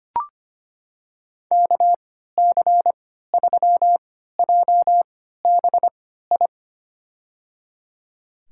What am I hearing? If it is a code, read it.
Morse code, heard as a keyed tone, about 25 wpm, spaced as for 14 wpm, 700 Hz. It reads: KC3JBI
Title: USA Call Signs - 3x Character Spacing - 25wpm